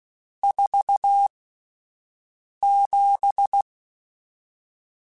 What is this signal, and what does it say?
Morse code, heard as a keyed tone, about 16 wpm, spaced as for 5 wpm, 800 Hz. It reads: 47
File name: Spaced Characters / Spaced Characters-mp3s / Weather5 5 sp @10